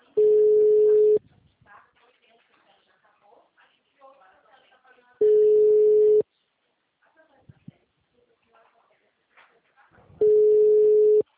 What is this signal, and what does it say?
Ringback tone (Europe, 425 Hz), with speech.
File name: out- -2024- - - .40344